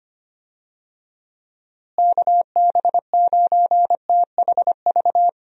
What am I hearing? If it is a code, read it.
Morse code, heard as a keyed tone, about 25 wpm, 700 Hz. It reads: KB9THV
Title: USA Call Signs - 6 Characters - Repeated 2x - 25wpm